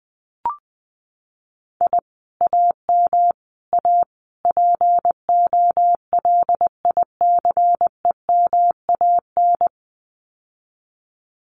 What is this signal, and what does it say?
Morse code, heard as a keyed tone, about 20 wpm, 700 Hz. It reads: I AM A POLICEMAN
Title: Common QSO Phrases Encore 20wpm